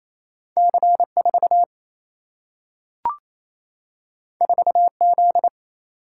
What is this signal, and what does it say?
Morse code, heard as a keyed tone, about 28 wpm, 700 Hz. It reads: C4 4Z
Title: Single-Letter Single-Number 28wpm